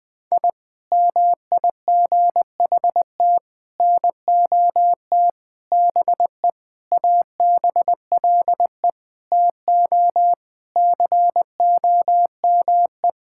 Morse code, heard as a keyed tone, about 20 wpm, 700 Hz. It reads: I MIGHT NOT BE ABLE TO COME
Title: Sentences with 4-Word Phrases - 20wpm